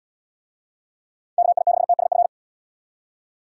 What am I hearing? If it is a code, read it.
Morse code, heard as a keyed tone, about 77 wpm, 700 Hz. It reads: KI6TSF